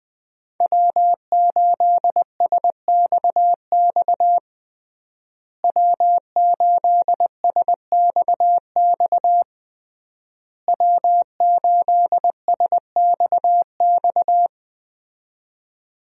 Morse code, heard as a keyed tone, about 20 wpm, 700 Hz. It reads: W8SXX W8SXX W8SXX
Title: USA Call Signs - 5 Characters - Repeated 3x - 20wpm